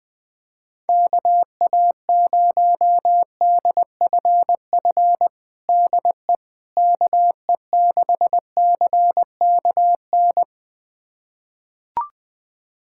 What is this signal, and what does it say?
Morse code, heard as a keyed tone, about 21 wpm, 700 Hz. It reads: KA0DFF DE KE6CKN
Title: USA Callsign DE Callsign 20wpm